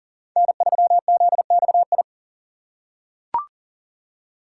Morse code, heard as a keyed tone, about 40 wpm, 700 Hz. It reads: N3ZXI
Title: USA Call Signs - 5 Characters - 6x Character Spacing - 40wpm